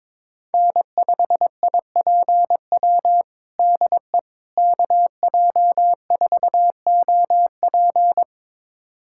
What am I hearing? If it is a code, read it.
Morse code, heard as a keyed tone, about 22 wpm, 700 Hz. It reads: N5IPW DE KJ4OP
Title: USA Callsign DE Callsign - 5 Characters - 22wpm